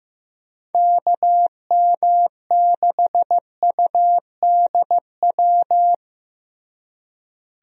Morse code, heard as a keyed tone, about 15 wpm, 700 Hz. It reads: KM6UDW
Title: USA Call Signs - 6 Characters - Repeated 3x - 15wpm